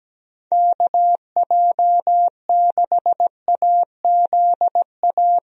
Morse code, heard as a keyed tone, about 17 wpm, 700 Hz. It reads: KJ6AZA
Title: USA Call Signs - 6 Characters - Repeated 3x - 17wpm